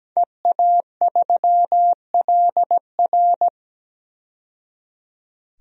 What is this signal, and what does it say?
Morse code, heard as a keyed tone, about 17 wpm, 700 Hz. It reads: EA3LR
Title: DXCC-100 Call Signs - 5 Characters - 17wpm